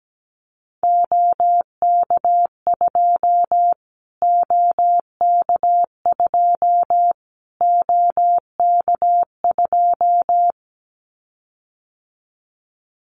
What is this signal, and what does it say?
Morse code, heard as a keyed tone, about 17 wpm, 700 Hz. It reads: OK2 OK2 OK2
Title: Callsign Prefixes for Top 25 DXCC - Repeated 3x - 17wpm